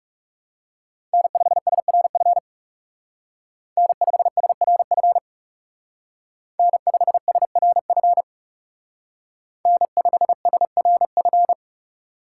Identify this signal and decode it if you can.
Morse code, heard as a keyed tone, about 38 wpm, 700 Hz. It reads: N5SRF N5SRF N5SRF N5SRF